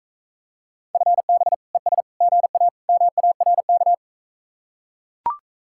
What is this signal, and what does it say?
Morse code, heard as a keyed tone, about 41 wpm, 700 Hz. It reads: FB ES GA MARK